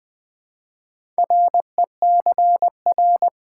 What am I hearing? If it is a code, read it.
Morse code, heard as a keyed tone, about 20 wpm, 700 Hz. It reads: RECR
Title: Build A Word - 4 Letters At A Time - 11-Letter Words - 20wpm